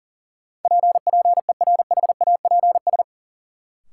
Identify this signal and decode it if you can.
Morse code, heard as a keyed tone, about 37 wpm, 700 Hz. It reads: P PERHAPS
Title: Sentences from Top 1100 Words - New Word Drill Alt - 40wpm